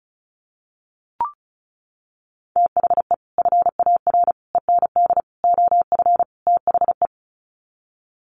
Morse code, heard as a keyed tone, about 34 wpm, 700 Hz. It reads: THE FAR END OF THE
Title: Sets of 5 Words 35wpm